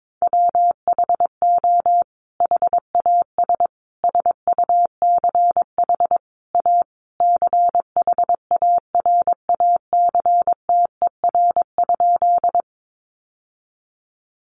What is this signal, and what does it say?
Morse code, heard as a keyed tone, about 22 wpm, 700 Hz. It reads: WHO HAS SUCH A CHARACTER?
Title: Sentences from Top 1200 Words 22wpm